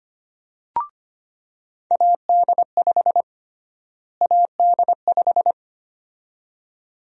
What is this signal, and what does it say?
Morse code, heard as a keyed tone, about 25 wpm, 700 Hz. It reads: AD5 AD5
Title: USA Call Sign Prefixes - Repeated 2x - 25wpm